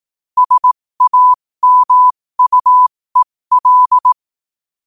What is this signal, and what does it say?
Morse code, heard as a keyed tone, about 18 wpm, spaced as for 15 wpm, 1000 Hz. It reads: SAMUEL
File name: qotdfiles / 15wpm